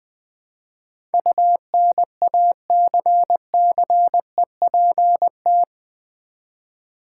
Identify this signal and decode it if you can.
Morse code, heard as a keyed tone, about 20 wpm, 700 Hz. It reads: UNACCEPT
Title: Build A Word - 4 Letters At A Time - 12-Letter Words - 20wpm